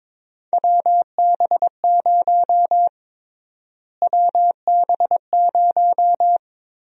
Morse code, heard as a keyed tone, about 22 wpm, 700 Hz. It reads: WB0 WB0
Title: USA Call Sign Prefixes - Repeated 2x - 22wpm